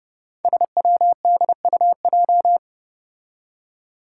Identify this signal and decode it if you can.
Morse code, heard as a keyed tone, about 32 wpm, 700 Hz. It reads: SWDUJ
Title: Five Challenging Letter-Number Combinations 30wpm